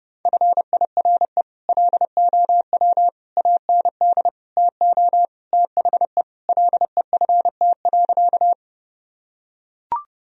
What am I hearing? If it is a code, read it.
Morse code, heard as a keyed tone, about 32 wpm, 700 Hz. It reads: FIRE LOW AND TO THE LEFT.
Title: Sentences from Top 1000 Words - Review Part 1 - 30wpm